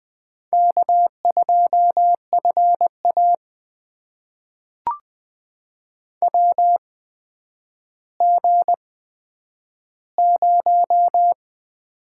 Morse code, heard as a keyed tone, about 20 wpm, spaced as for 10 wpm, 700 Hz. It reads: K2FA WG0
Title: USA Call Signs - 4 Characters - 8x Character Spacing - 20wpm